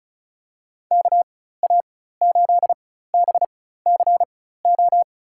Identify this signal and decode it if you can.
Morse code, heard as a keyed tone, about 36 wpm, spaced as for 16 wpm, 700 Hz. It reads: KA8BCO
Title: USA Call Signs - 6 Characters - 4x Character Spacing - 35wpm